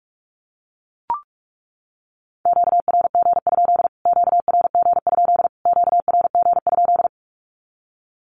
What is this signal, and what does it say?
Morse code, heard as a keyed tone, about 44 wpm, 700 Hz. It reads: QRG? QRG? QRG?
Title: Q-Codes Repeated 3x 45wpm